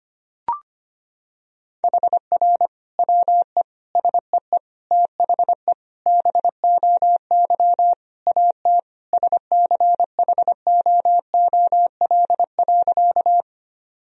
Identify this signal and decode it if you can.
Morse code, heard as a keyed tone, about 25 wpm, 700 Hz. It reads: HR WE SEE THE BOY AT SCHOOL.
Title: Sentences from Top 300 Words and Abbreviations 25wpm